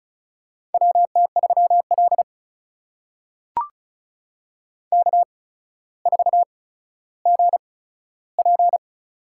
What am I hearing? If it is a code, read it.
Morse code, heard as a keyed tone, about 36 wpm, spaced as for 15 wpm, 700 Hz. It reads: WT3L K4GP
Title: USA Call Signs - 8x Character Spacing - 35wpm